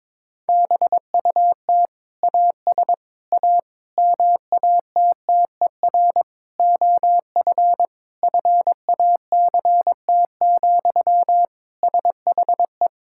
Morse code, heard as a keyed tone, about 22 wpm, 700 Hz. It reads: BUT AS A MATTER OF FACT, SHE
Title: Sentences from Top 1000 Words - Review Part 6 - 22wpm